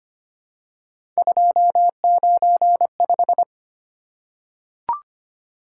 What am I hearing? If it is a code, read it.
Morse code, heard as a keyed tone, about 25 wpm, 700 Hz. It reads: 295
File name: Three Letter-Number Combinations - 25wpm